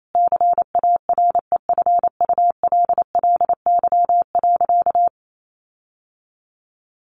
Morse code, heard as a keyed tone, about 28 wpm, 700 Hz. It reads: CAREFULLY.
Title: Sentences from Top 1500 Words 28wpm